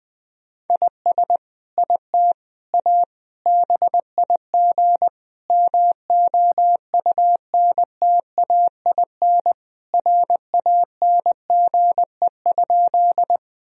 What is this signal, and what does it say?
Morse code, heard as a keyed tone, about 20 wpm, 700 Hz. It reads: IS IT A BIG MOUNTAIN RANGE?
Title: Sentences from Top 1000 Words - Review Part 4 - 20wpm